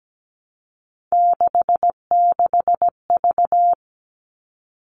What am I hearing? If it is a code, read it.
Morse code, heard as a keyed tone, about 17 wpm, 700 Hz. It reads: 66V
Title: All Cut Numbers - 3-Digit 17wpm